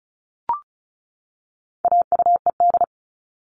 Morse code, heard as a keyed tone, about 34 wpm, 700 Hz. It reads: AUED
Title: All Cut Numbers - 4-Digit 35wpm